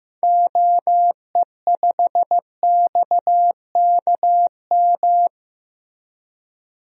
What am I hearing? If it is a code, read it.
Morse code, heard as a keyed tone, about 15 wpm, 700 Hz. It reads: OE5XKM
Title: DXCC-100 Call Signs - 6 Characters - Repeated 3x - 15wpm